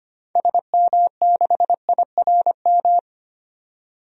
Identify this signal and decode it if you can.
Morse code, heard as a keyed tone, about 25 wpm, 700 Hz. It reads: SM6IRM
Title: DXCC-100 Call Signs - 6 Characters - 25wpm